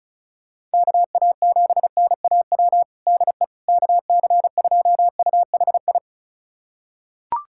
Morse code, heard as a keyed tone, about 36 wpm, 700 Hz. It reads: KA7NAW DE KC2UHI
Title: USA Callsign DE Callsign 35wpm